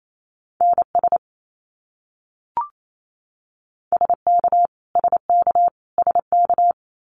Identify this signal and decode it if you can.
Morse code, heard as a keyed tone, about 28 wpm, 700 Hz. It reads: NS SK SK SK
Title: Canadian Provinces and Territories - Repeated 3x - 28wpm